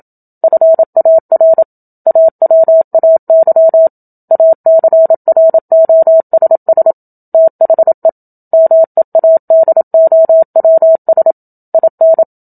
Morse code, heard as a keyed tone, about 27 wpm, 650 Hz. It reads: FAR AWAY ACROSS THE MEADOWS IN
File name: Book Two / 01-Chapter 1 - 27 WPM